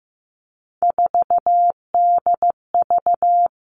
Morse code, heard as a keyed tone, about 15 wpm, 700 Hz. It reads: 4DV
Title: Three Challenging Letter-Number Combinations 15wpm